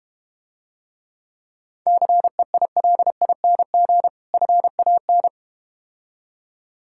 Morse code, heard as a keyed tone, about 32 wpm, 700 Hz. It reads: CEILING FAN